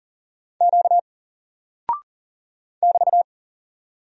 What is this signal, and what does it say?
Morse code, heard as a keyed tone, about 40 wpm, 700 Hz. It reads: Q <BT>
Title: Single Character (Letter:Number:Punctuation:Prosign) 40wpm